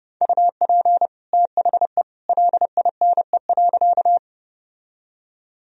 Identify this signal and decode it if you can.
Morse code, heard as a keyed tone, about 30 wpm, 700 Hz. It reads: UP THE LINE.
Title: Sentences from Top 1000 Words - Review Part 1 - 30wpm